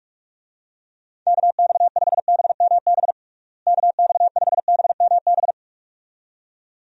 Morse code, heard as a keyed tone, about 42 wpm, 700 Hz. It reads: KX5BMB KX5BMB
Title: USA Call Signs - 6 Characters - Repeated 2x - 45wpm